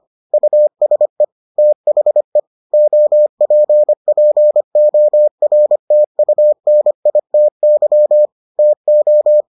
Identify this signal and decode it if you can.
Morse code, heard as a keyed tone, about 25 wpm, 600 Hz. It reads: USE THE OPPORTUNITY TO